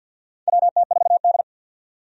Morse code, heard as a keyed tone, about 52 wpm, 700 Hz. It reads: WT4D
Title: USA Call Signs - 4 Characters - 50wpm